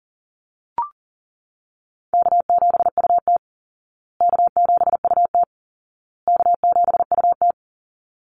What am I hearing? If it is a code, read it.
Morse code, heard as a keyed tone, about 40 wpm, 700 Hz. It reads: K7UT K7UT K7UT
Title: Easy Call Signs Repeated 3x - 40wpm